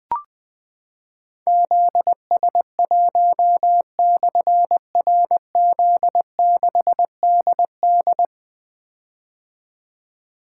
Morse code, heard as a keyed tone, about 20 wpm, 700 Hz. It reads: ZS1/RZ6DD
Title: Difficult Call Signs 20wpm